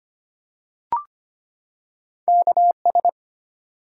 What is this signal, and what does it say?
Morse code, heard as a keyed tone, about 25 wpm, 700 Hz. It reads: KS
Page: morse-code-ninja-practice-25wpm